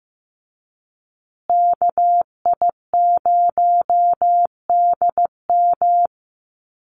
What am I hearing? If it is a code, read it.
Morse code, heard as a keyed tone, about 15 wpm, 700 Hz. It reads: KI0DM
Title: USA Call Signs Repeated 3x 15wpm